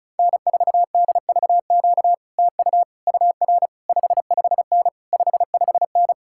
Morse code, heard as a keyed tone, about 37 wpm, 700 Hz. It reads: N4DVQ TU UR 55N 55N
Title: Speed-Racing Common QSO Phrases - 30wpm